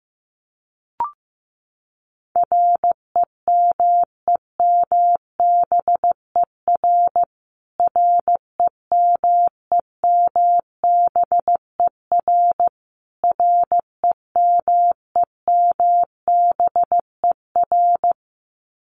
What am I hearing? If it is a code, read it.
Morse code, heard as a keyed tone, about 15 wpm, 700 Hz. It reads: REMEMBER REMEMBER REMEMBER
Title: Top 500 Words Repeated 15wpm